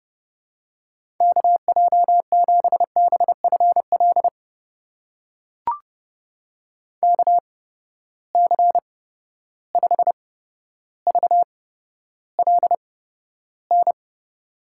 Morse code, heard as a keyed tone, about 30 wpm, spaced as for 13 wpm, 700 Hz. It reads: KJ7BFL KC5VLN
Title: USA Call Signs - 6 Characters - 8x Character Spacing - 30wpm